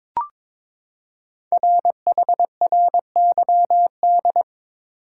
Morse code, heard as a keyed tone, about 22 wpm, 700 Hz. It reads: RHRYD
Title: Code Groups - 5 Letters - 22wpm